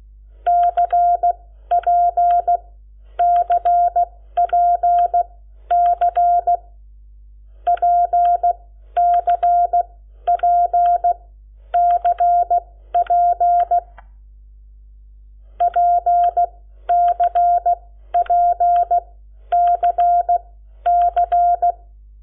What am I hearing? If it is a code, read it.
Morse code, heard as a keyed tone, about 16 wpm, spaced as for 11 wpm, 700 Hz. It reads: CPCPC PCPCP PCPCC
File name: Lecciones / CLASE Nº7